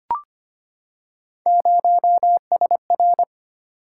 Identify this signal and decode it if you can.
Morse code, heard as a keyed tone, about 25 wpm, 700 Hz. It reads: 0SR
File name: Three Letter-Number Combinations - 25wpm